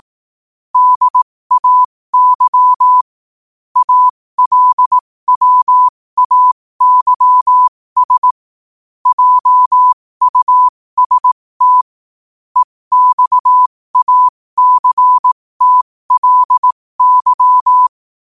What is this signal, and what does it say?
Morse code, heard as a keyed tone, about 18 wpm, spaced as for 15 wpm, 1000 Hz. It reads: DAY ALWAYS JUST EXACTLY
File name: qotdfiles / 15wpm